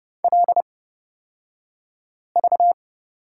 Morse code, heard as a keyed tone, about 31 wpm, spaced as for 5 wpm, 700 Hz. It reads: LV
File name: Lesson 38d - Letter q - Two Characters 30wpm